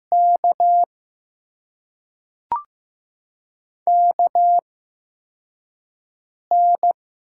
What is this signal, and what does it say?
Morse code, heard as a keyed tone, about 15 wpm, spaced as for 4 wpm, 700 Hz. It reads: K KN
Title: USA Call Signs - 6 Characters - 8x Character Spacing - 15wpm